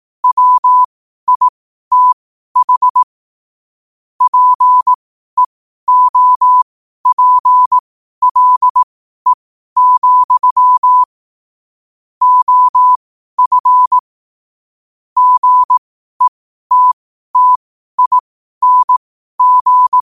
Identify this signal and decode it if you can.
Morse code, heard as a keyed tone, about 18 wpm, spaced as for 12 wpm, 1000 Hz. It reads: WITH PEOPLE, OF GETTING
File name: qotdfiles / 12wpm